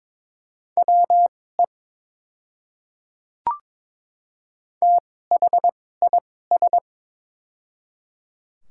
Morse code, heard as a keyed tone, about 22 wpm, spaced as for 16 wpm, 700 Hz. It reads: WE THIS
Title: 2x Character Spacing - Top 100 Words - 22wpm